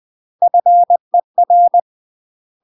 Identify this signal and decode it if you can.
Morse code, heard as a keyed tone, about 20 wpm, 700 Hz. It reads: FER
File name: UK MASTER QSOs TRAINING / 20WPM / UK MASTER QSOs 0016